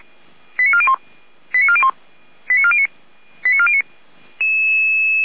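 Five-tone selective call (ZVEI), heard as quick runs of five tones, tones of about 70 ms. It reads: 89401 89401 89409 89409
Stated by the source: ZVEI five-tone selective call received on SDR.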